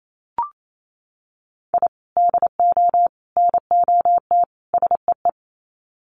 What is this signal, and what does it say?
Morse code, heard as a keyed tone, about 28 wpm, 700 Hz. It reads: I DO NOT SEE
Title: Sets of 4 Words 28wpm